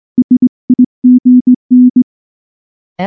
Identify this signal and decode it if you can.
Morse code, heard as a keyed tone, about 19 wpm, 250 Hz. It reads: SIGN
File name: morse / morse-training-20WPM-5